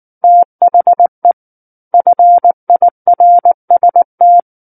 Morse code, heard as a keyed tone, about 19 wpm, 700 Hz. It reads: THE FIRST
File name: Book One / 10-Chapter 10 - 19 WPM